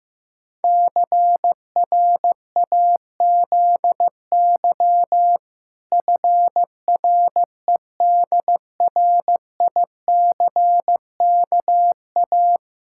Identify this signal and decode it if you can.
Morse code, heard as a keyed tone, about 14 wpm, 700 Hz. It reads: CRAZY FREDRICKA
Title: Pangram Challenge v2 - 15wpm